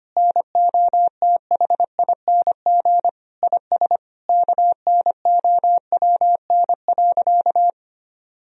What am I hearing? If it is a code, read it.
Morse code, heard as a keyed tone, about 25 wpm, 700 Hz. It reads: NOTHING IS KNOWN.
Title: Sentences from Top 1000 Words - Review Part 4 - 25wpm